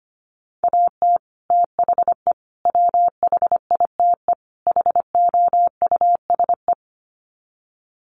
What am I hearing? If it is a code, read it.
Morse code, heard as a keyed tone, about 25 wpm, 700 Hz. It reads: AT THE WHITE HOUSE
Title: Sets of 4 Words 25wpm